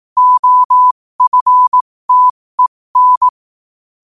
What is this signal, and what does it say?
Morse code, heard as a keyed tone, about 18 wpm, spaced as for 15 wpm, 1000 Hz. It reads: OFTEN
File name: qotdfiles / 15wpm